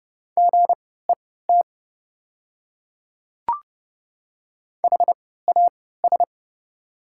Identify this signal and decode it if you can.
Morse code, heard as a keyed tone, about 30 wpm, spaced as for 17 wpm, 700 Hz. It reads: GET HAS
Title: 3x Character Spacing - Top 200 Words - 30wpm